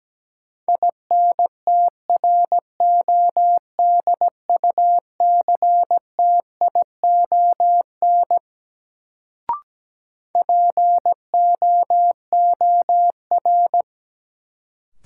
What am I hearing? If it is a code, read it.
Morse code, heard as a keyed tone, about 17 wpm, 700 Hz. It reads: INTRODUCTION POOR
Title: Sentences from Top 1000 Words - New Word Drill - 17wpm